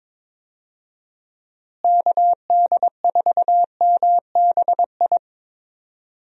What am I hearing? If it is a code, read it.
Morse code, heard as a keyed tone, about 22 wpm, 700 Hz. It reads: KD4MBI
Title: USA Call Signs - 6 Characters - Repeated 2x - 22wpm